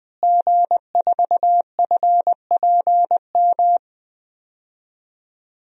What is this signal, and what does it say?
Morse code, heard as a keyed tone, about 21 wpm, 700 Hz. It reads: G4FPM
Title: DXCC-100 Call Signs - 5 Characters - Repeated 2x - 20wpm